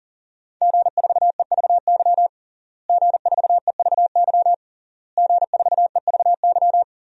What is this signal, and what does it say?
Morse code, heard as a keyed tone, about 40 wpm, 700 Hz. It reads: G4EVY G4EVY G4EVY
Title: DXCC-100 Call Signs - 5 Characters - Repeated 3x - 40wpm